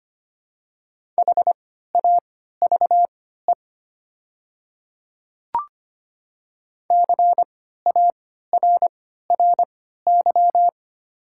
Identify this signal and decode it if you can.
Morse code, heard as a keyed tone, about 25 wpm, spaced as for 14 wpm, 700 Hz. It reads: HAVE CARRY
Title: 3x Character Spacing - Top 500 Words - 25wpm